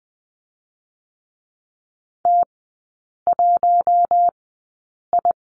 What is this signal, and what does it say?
Morse code, heard as a keyed tone, about 20 wpm, spaced as for 8 wpm, 700 Hz. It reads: T1I
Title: 053 Lesson 10e - Letter r - Three Characters 20wpm